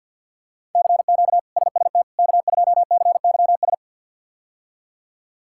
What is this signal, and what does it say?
Morse code, heard as a keyed tone, about 46 wpm, 700 Hz. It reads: CQ SST K2CYS
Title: K1USN SST QSO Practice - 50wpm